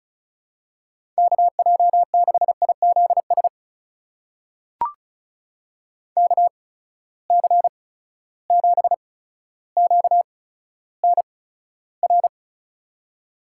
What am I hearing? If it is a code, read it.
Morse code, heard as a keyed tone, about 36 wpm, spaced as for 15 wpm, 700 Hz. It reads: KJ6IZS KC7QNR
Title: USA Call Signs - 8x Character Spacing - 35wpm